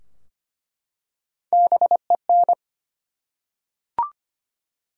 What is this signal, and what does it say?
Morse code, heard as a keyed tone, about 25 wpm, 700 Hz. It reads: BEN
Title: USA Common Names - Max 3 Letters - 25wpm